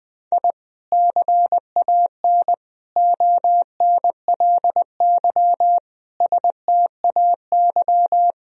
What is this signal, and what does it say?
Morse code, heard as a keyed tone, about 20 wpm, 700 Hz. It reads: I CAN ONLY STAY
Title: Sentences from Top 1000 Words - Review Part 3 - 20wpm